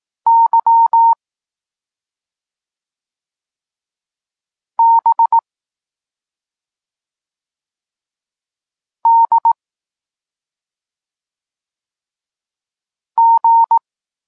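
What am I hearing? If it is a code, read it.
Morse code, heard as a keyed tone, about 18 wpm, 920 Hz. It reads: Y B D G
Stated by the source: Polish Morse lesson teaching the letter C, from SP9MOA Niepołomicki Klub Krótkofalowców.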